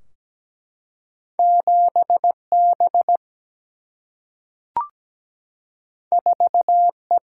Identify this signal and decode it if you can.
Morse code, heard as a keyed tone, about 17 wpm, 700 Hz. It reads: 7B 4E
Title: Single-Letter Single-Number 17wpm